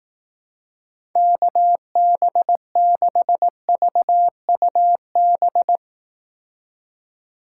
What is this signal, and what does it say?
Morse code, heard as a keyed tone, about 18 wpm, 700 Hz. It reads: KB6VUB
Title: Speed Racing - USA 2x3 Call Signs 15wpm